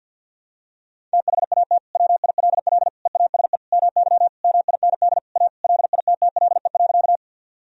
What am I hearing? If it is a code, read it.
Morse code, heard as a keyed tone, about 47 wpm, 700 Hz. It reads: THAT WILL EASE MY MIND A LITTLE.
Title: Sentences from Top 1000 Words - Review Part 1 - 50wpm